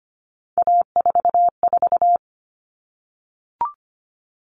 Morse code, heard as a keyed tone, about 25 wpm, 700 Hz. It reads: A44
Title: All Cut Numbers - 3-Digit 25wpm